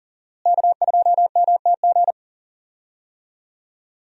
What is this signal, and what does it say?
Morse code, heard as a keyed tone, about 40 wpm, 700 Hz. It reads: K2MTG
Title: USA Call Signs - 5 Characters - Repeated 3x - 40wpm